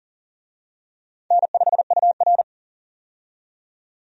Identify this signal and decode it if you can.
Morse code, heard as a keyed tone, about 40 wpm, 700 Hz. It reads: N5UR
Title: USA Call Signs - 4 Characters - 40wpm